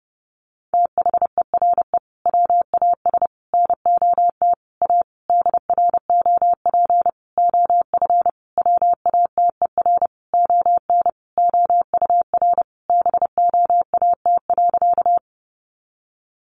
Morse code, heard as a keyed tone, about 28 wpm, 700 Hz. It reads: THERE WAS NOT A DROP OF WATER ON OUR BOAT.
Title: Sentences from Top 1400 Words 30wpm